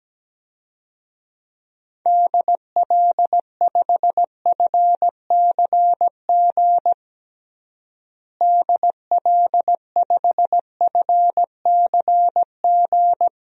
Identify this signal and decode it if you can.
Morse code, heard as a keyed tone, about 17 wpm, 700 Hz. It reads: DL5FCG DL5FCG
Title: DXCC-100 Call Signs - 6 Characters - Repeated 2x - 17wpm